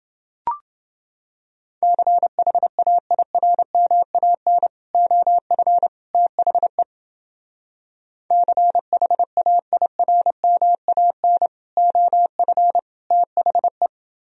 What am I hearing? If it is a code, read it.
Morse code, heard as a keyed tone, about 28 wpm, 700 Hz. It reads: CHAIRMAN OF THE CHAIRMAN OF THE